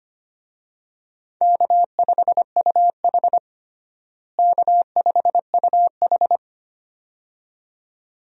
Morse code, heard as a keyed tone, about 25 wpm, 700 Hz. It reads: K5UH K5UH
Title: USA Call Signs - 4 Characters - Repeated 2x - 25wpm